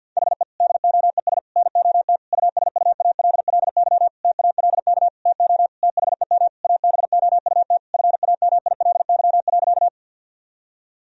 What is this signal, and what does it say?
Morse code, heard as a keyed tone, about 52 wpm, 700 Hz. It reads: HE DOES NOT USUALLY TALK TO THEM ABOUT FAMILY.
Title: Sentences from Top 1000 Words - Review Part 3 - 50wpm